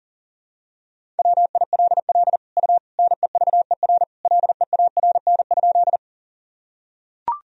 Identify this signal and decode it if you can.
Morse code, heard as a keyed tone, about 40 wpm, 700 Hz. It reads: WILL U NEVER LEARN?
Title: Sentences from Top 300 Words and Abbreviations 40wpm